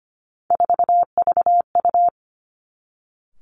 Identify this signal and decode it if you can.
Morse code, heard as a keyed tone, about 25 wpm, 700 Hz. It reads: 4VU
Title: Three Challenging Letter-Number Combinations 25wpm